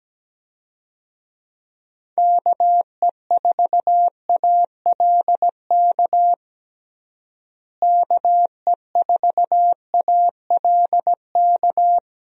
Morse code, heard as a keyed tone, about 17 wpm, 700 Hz. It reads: KE4ALK KE4ALK
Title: USA Call Signs - 6 Characters - Repeated 3x - 17wpm